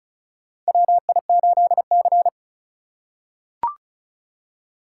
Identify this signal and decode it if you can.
Morse code, heard as a keyed tone, about 36 wpm, 700 Hz. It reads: WI8C
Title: USA Call Signs - 4 Characters - 4x Character Spacing - 35wpm